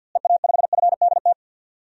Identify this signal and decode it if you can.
Morse code, heard as a keyed tone, about 51 wpm, 700 Hz. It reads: EA5FDT
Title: DXCC-100 Call Signs - 6 Characters - Repeated 2x - 50wpm